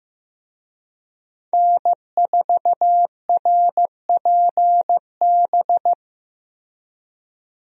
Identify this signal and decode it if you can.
Morse code, heard as a keyed tone, about 15 wpm, 700 Hz. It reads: N4RPB
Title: USA Call Signs - 5 Characters - Repeated 3x - 15wpm